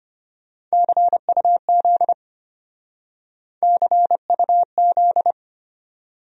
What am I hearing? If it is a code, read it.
Morse code, heard as a keyed tone, about 32 wpm, 700 Hz. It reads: CUZ CUZ